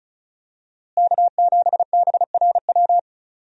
Morse code, heard as a keyed tone, about 34 wpm, 700 Hz. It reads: K7BRW
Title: USA Call Signs - 5 Characters - Repeated 3x - 35wpm